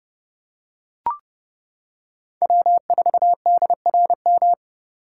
Morse code, heard as a keyed tone, about 29 wpm, 700 Hz. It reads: W4DRM